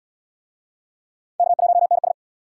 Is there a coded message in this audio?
Yes: Morse code from 1.4 to 2.1 s, heard as a keyed tone, about 71 wpm, 700 Hz. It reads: K1NS